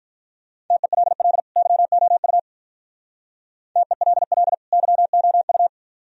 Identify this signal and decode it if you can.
Morse code, heard as a keyed tone, about 43 wpm, 700 Hz. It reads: TELL YOU TELL YOU
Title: Speed-Racing - Sets of 2 Words 35wpm